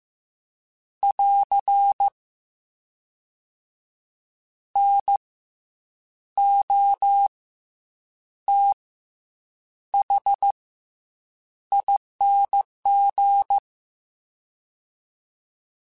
Morse code, heard as a keyed tone, about 15 wpm, 800 Hz. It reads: <AR> N O T H ING